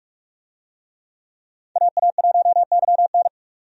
Morse code, heard as a keyed tone, about 46 wpm, 700 Hz. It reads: AA1YN
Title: USA Call Signs - 5 Characters - 45wpm